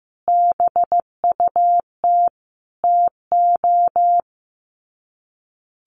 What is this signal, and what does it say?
Morse code, heard as a keyed tone, about 15 wpm, 700 Hz. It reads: BUT TO
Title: Sets of 5 Words 15wpm